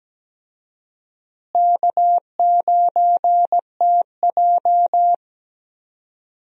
Morse code, heard as a keyed tone, about 17 wpm, 700 Hz. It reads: K9TJ
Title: USA Call Signs - 4 Characters - Repeated 2x - 17wpm